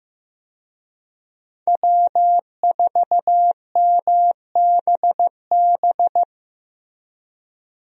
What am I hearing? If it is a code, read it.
Morse code, heard as a keyed tone, about 16 wpm, 700 Hz. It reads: W4MBB
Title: USA Call Signs - 5 Characters - Repeated 2x - 15wpm